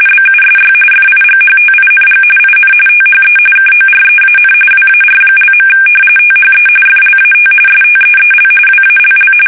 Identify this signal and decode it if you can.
RTTY, about 45.45 baud. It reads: BUSY FIGHTING IN THE KOREAN WAR). THE FLAT STEEL WALL
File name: fsk850 / battleshipnj